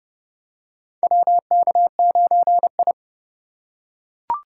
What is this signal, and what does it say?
Morse code, heard as a keyed tone, about 30 wpm, 700 Hz. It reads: WK9I
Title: USA Call Signs - 4x Character Spacing - 30wpm